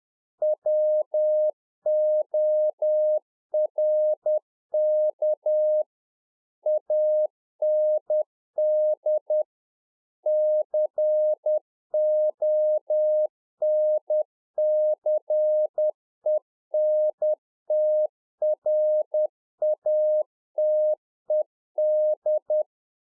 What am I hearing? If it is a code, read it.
Morse code, heard as a keyed tone, about 10 wpm, 610 Hz. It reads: WORK AND CONCENTRATED